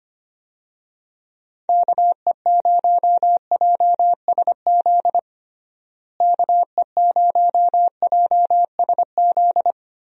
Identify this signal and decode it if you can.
Morse code, heard as a keyed tone, about 25 wpm, 700 Hz. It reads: KE0JSZ KE0JSZ
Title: USA Call Signs - 6 Characters - Repeated 2x - 25wpm